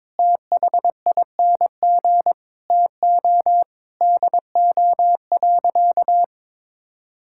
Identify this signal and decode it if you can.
Morse code, heard as a keyed tone, about 22 wpm, 700 Hz. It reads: THING TO DO.
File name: practice / 22wpm / Sentences with 8-Word Phrases - 22wpm